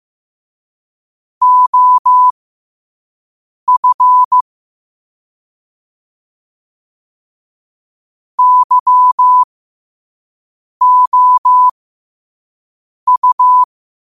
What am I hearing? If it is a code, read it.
Morse code, heard as a keyed tone, about 15 wpm, spaced as for 5 wpm, 1000 Hz. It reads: OF YOU
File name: qotdfiles / 5wpm